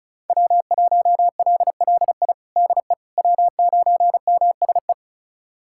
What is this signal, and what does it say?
Morse code, heard as a keyed tone, about 36 wpm, 700 Hz. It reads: W1LLI DE W9MSE
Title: USA Callsign DE Callsign - 5 Characters - 35wpm